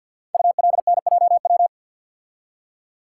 Morse code, heard as a keyed tone, about 47 wpm, 690 Hz. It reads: ULNJW
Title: Code Groups - 5 Letters - 50wpm